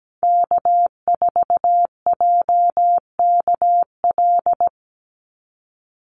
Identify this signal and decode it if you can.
Morse code, heard as a keyed tone, about 17 wpm, 700 Hz. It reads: K4JKL
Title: Call Signs Repeated 3x - 17wpm